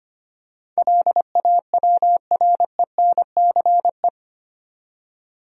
Morse code, heard as a keyed tone, about 25 wpm, 700 Hz. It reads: LAWRENCE
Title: USA Common Names - 25wpm